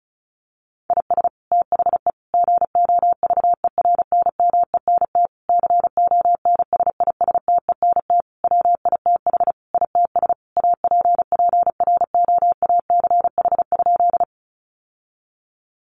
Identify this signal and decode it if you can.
Morse code, heard as a keyed tone, about 34 wpm, 700 Hz. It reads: IS THE GOVERNMENT CONSISTENT WITH ITS APPROACH?
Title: Sentences from Top 1500 Words 35wpm